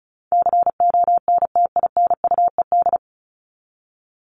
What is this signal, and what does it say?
Morse code, heard as a keyed tone, about 34 wpm, 700 Hz. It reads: CONTINUED
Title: ED Suffix - Top 100 Words with Base Words 35wpm